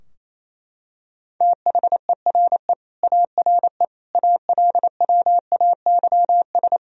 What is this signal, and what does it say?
Morse code, heard as a keyed tone, about 28 wpm, 700 Hz. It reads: THERE ARE ALWAYS
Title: Sentences from Top 1000 Words - Review Part 4 - 28wpm